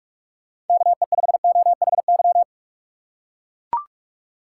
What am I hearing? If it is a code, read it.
Morse code, heard as a keyed tone, about 44 wpm, 700 Hz. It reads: KE5OHY
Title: USA Call Signs - 6 Characters - 3x Character Spacing - 45wpm